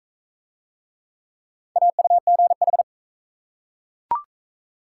Morse code, heard as a keyed tone, about 36 wpm, 700 Hz. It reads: AUGH